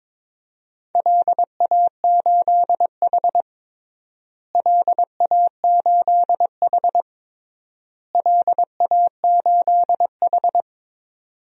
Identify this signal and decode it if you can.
Morse code, heard as a keyed tone, about 22 wpm, 700 Hz. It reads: LA8H LA8H LA8H
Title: DXCC-100 Call Signs - 4 Characters - Repeated 3x - 22wpm